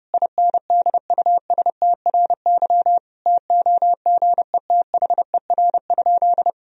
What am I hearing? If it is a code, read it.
Morse code, heard as a keyed tone, about 28 wpm, 700 Hz. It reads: INDUSTRY TOGETHER?
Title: Sentences from Top 1000 Words - Review Part 4 - 30wpm